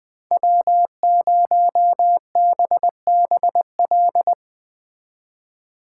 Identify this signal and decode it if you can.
Morse code, heard as a keyed tone, about 22 wpm, 700 Hz. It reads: W0BBL
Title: Speed Racing - USA 1x3 Call Signs 20wpm